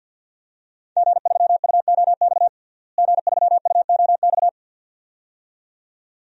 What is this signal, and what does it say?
Morse code, heard as a keyed tone, about 49 wpm, 700 Hz. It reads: G3UOX G3UOX
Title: DXCC-100 Call Signs - 5 Characters - Repeated 2x - 50wpm